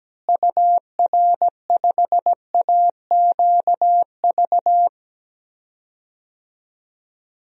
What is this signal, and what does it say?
Morse code, heard as a keyed tone, about 17 wpm, 700 Hz. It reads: UR5AQV
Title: DXCC-100 Call Signs - 6 Characters - Repeated 2x - 17wpm